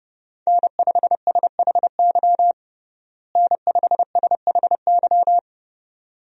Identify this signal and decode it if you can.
Morse code, heard as a keyed tone, about 30 wpm, 700 Hz. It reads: N5SHY N5SHY
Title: USA Call Signs - 5 Characters - Repeated 2x - 30wpm